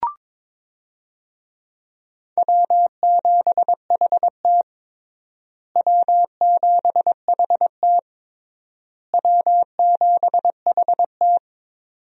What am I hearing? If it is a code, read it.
Morse code, heard as a keyed tone, about 22 wpm, 700 Hz. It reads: W7HT W7HT W7HT
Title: USA Call Signs - 4 Characters - Repeated 3x - 22wpm